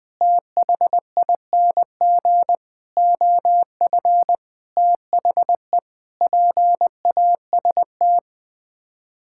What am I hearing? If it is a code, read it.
Morse code, heard as a keyed tone, about 20 wpm, 700 Hz. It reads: THING OF THE PAST
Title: Sets of 6 Words - 20wpm